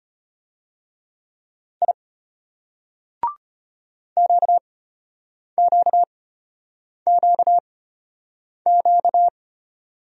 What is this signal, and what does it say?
Morse code, heard as a keyed tone, about 33 wpm, spaced as for 8 wpm, 700 Hz. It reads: I QQQQ